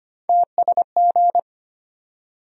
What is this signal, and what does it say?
Morse code, heard as a keyed tone, about 25 wpm, 700 Hz. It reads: TSG
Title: USA Call Sign Postfixes - 3 Letters 25wpm